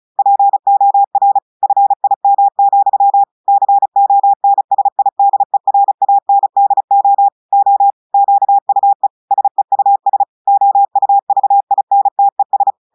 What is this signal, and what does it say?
Morse code, heard as a keyed tone, about 35 wpm, 800 Hz. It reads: POR FIM, CONSIDERANDO O QUE SEUS OUVINTES
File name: portugais texte en clair 2 vitesse 35 mots minutes